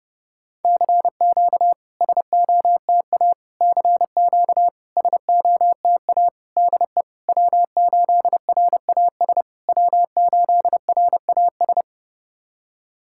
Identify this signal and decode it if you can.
Morse code, heard as a keyed tone, about 30 wpm, 700 Hz. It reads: CQ SOTA CQ SOTA DE W8RAS W8RAS
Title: SOTA QSO Practice 30wpm